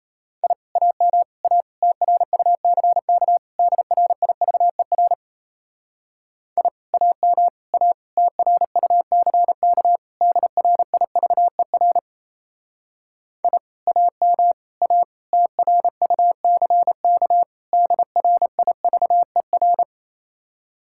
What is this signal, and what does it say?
Morse code, heard as a keyed tone, about 35 wpm, 700 Hz. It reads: I AM A TRUCK DRIVER I AM A TRUCK DRIVER I AM A TRUCK DRIVER